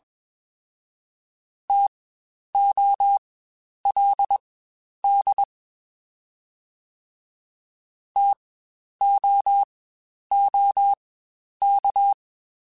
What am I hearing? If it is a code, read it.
Morse code, heard as a keyed tone, about 21 wpm, spaced as for 10 wpm, 800 Hz. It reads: TOLD TOOK
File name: Spaced Alphas / Spaced Alphas mp3s / Ts sp @20